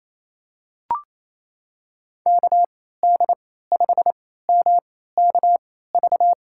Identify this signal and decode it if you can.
Morse code, heard as a keyed tone, about 27 wpm, spaced as for 16 wpm, 700 Hz. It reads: KD5MKV
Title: USA Call Signs - 3x Character Spacing - 28wpm